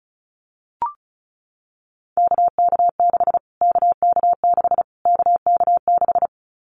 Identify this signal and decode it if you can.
Morse code, heard as a keyed tone, about 34 wpm, 700 Hz. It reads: KK6 KK6 KK6
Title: Callsign Prefixes for Top 25 DXCC - Repeated 3x - 35wpm